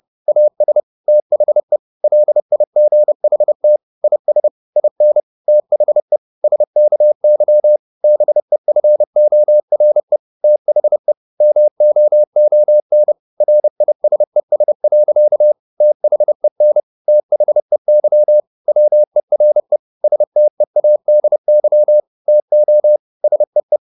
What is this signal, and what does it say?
Morse code, heard as a keyed tone, about 30 wpm, 600 Hz. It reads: AS THE LIGHT IS IN THE SKY BEFORE THE MOON RISES. THEN THEY WERE STEADY TO SEE